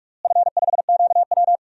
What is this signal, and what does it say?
Morse code, heard as a keyed tone, about 45 wpm, 700 Hz. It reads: F5QW
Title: DXCC-100 Call Signs - 4 Characters - Repeated 3x - 45wpm